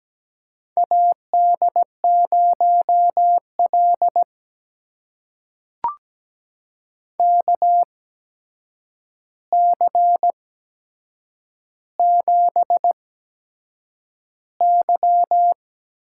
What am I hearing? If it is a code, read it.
Morse code, heard as a keyed tone, about 17 wpm, spaced as for 7 wpm, 700 Hz. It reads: AD0L KC7Y
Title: USA Call Signs - 8x Character Spacing - 17wpm